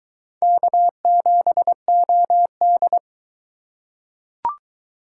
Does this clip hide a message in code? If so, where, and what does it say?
Yes: Morse code from 0.4 to 3.0 s, heard as a keyed tone, about 22 wpm, 700 Hz. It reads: K7OD